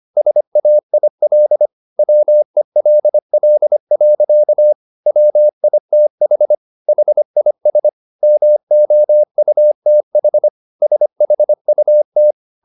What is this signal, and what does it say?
Morse code, heard as a keyed tone, about 25 wpm, 600 Hz. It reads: SAIL WELL. WITH HIS MOUTH SHUT